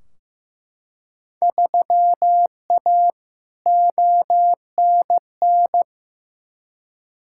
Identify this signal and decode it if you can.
Morse code, heard as a keyed tone, about 15 wpm, 700 Hz. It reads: 3A ONN
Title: Field Day Exchanges Encore - 15wpm